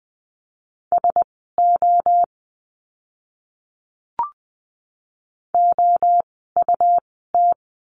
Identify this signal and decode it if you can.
Morse code, heard as a keyed tone, about 20 wpm, spaced as for 14 wpm, 700 Hz. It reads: SO OUT
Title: 115 Lesson 19f - Letter w - Words 20wpm